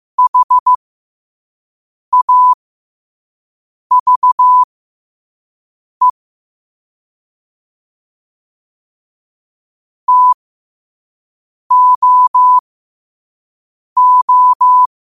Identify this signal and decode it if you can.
Morse code, heard as a keyed tone, about 15 wpm, spaced as for 5 wpm, 1000 Hz. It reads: HAVE TOO
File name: qotdfiles / 5wpm